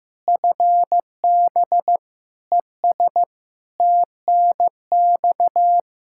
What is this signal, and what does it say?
Morse code, heard as a keyed tone, about 16 wpm, 700 Hz. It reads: FB ES TNX
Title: Practice QSO - Exchange 2 - Info - 15wpm